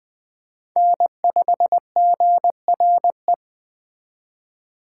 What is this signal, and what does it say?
Morse code, heard as a keyed tone, about 20 wpm, 700 Hz. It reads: N5GRE